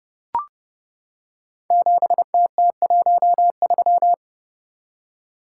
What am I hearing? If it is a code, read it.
Morse code, heard as a keyed tone, about 30 wpm, 700 Hz. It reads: 7TT13
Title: Code Groups - 5 Letter:Numbers - 30wpm